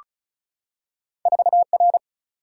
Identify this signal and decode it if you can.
Morse code, heard as a keyed tone, about 36 wpm, 700 Hz. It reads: 4R
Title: Single-Letter Single-Number 35wpm